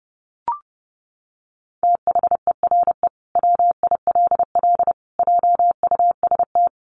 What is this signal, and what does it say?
Morse code, heard as a keyed tone, about 30 wpm, 700 Hz. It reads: THERE WILL JUST
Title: Sentences from Top 200 Words 30wpm